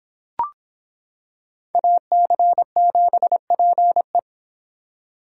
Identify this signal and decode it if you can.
Morse code, heard as a keyed tone, about 25 wpm, 700 Hz. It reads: AC7PE